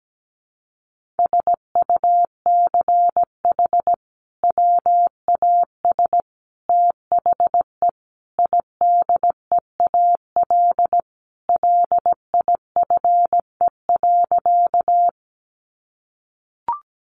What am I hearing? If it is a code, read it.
Morse code, heard as a keyed tone, about 17 wpm, 700 Hz. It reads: SUCH WAS THE IDEAL LIFE.
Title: Sentences from Top 1400 Words 17wpm